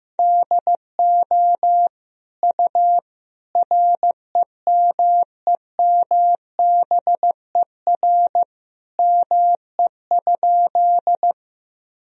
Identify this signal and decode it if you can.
Morse code, heard as a keyed tone, about 14 wpm, 700 Hz. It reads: DO U REMEMBER ME?
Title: Sentences from Top 400 Words and Abbreviations 15wpm